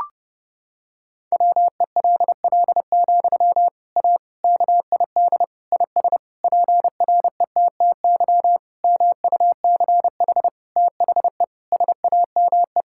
Morse code, heard as a keyed tone, about 32 wpm, 700 Hz. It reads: WELL, A KID IS PRETTY MUCH THE SAME
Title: Sentences from Top 1600 Words 30wpm